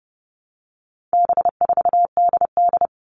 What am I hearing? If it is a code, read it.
Morse code, heard as a keyed tone, about 30 wpm, 700 Hz. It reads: B4DD
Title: All Cut Numbers - 4-Digit 30wpm